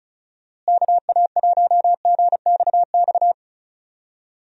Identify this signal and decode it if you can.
Morse code, heard as a keyed tone, about 36 wpm, 700 Hz. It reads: KA1GXX
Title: USA Call Signs - 6 Characters - Repeated 2x - 35wpm